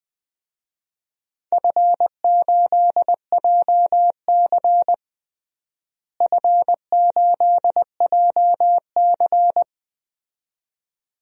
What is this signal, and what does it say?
Morse code, heard as a keyed tone, about 21 wpm, 700 Hz. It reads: F8JC F8JC
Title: DXCC-100 Call Signs - 4 Characters - Repeated 2x - 20wpm